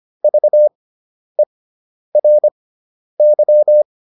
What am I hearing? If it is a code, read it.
Morse code, heard as a keyed tone, about 25 wpm, spaced as for 10 wpm, 600 Hz. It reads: VERY